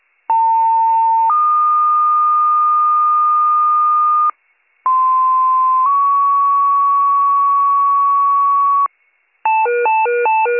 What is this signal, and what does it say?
Two-tone page, heard as a one-second tone, then a three-second tone, about 900 then 1220 Hz, about 1040 then 1120 Hz. It is heard from 0.3 to 8.9 s.